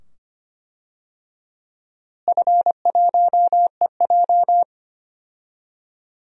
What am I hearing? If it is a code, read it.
Morse code, heard as a keyed tone, about 25 wpm, 700 Hz. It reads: F1EJ
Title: DXCC-100 Call Signs - 4 Characters - Repeated 2x - 25wpm